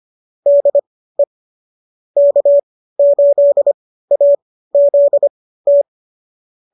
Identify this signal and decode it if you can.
Morse code, heard as a keyed tone, about 25 wpm, spaced as for 15 wpm, 590 Hz. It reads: DE K8AZT